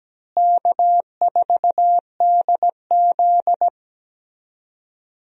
Morse code, heard as a keyed tone, about 17 wpm, 700 Hz. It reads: K4DZ
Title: USA Call Signs - 4 Characters - Repeated 2x - 17wpm